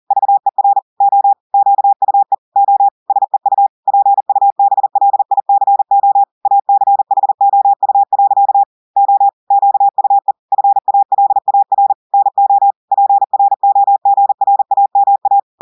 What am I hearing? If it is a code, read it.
Morse code, heard as a keyed tone, about 40 wpm, 800 Hz. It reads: VER O QUE O SEU PUBLICO ACHOU. O QUE FALAR NO PROGRAMA
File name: audios / portugais / portugais texte en clair 2 vitesse 40 mots minutes